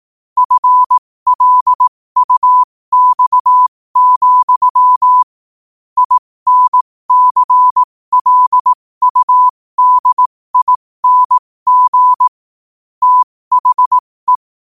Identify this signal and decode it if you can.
Morse code, heard as a keyed tone, about 18 wpm, spaced as for 15 wpm, 1000 Hz. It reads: FLUX, INCLUDING THE
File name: qotdfiles / 15wpm